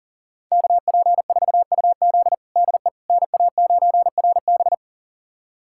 Morse code, heard as a keyed tone, about 38 wpm, 700 Hz. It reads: KP4UZ DE NA9RB
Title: USA Callsign DE Callsign - 5 Characters - 40wpm